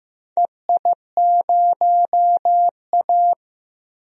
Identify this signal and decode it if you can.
Morse code, heard as a keyed tone, about 15 wpm, 700 Hz. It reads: EI0A
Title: DXCC-100 Call Signs - 4 Characters - 15wpm